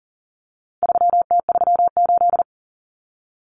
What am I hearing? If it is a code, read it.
Morse code, heard as a keyed tone, about 40 wpm, 700 Hz. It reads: 3T38
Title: Common Cut Numbers - 4-Digit 40wpm